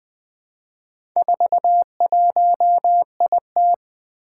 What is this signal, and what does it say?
Morse code, heard as a keyed tone, about 20 wpm, 700 Hz. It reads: 41IT
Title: Four Letter-Number Combinations - 20wpm